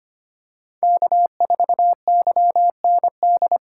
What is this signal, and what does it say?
Morse code, heard as a keyed tone, about 25 wpm, 700 Hz. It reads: K4YND
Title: K1USN SST QSO Practice - 25wpm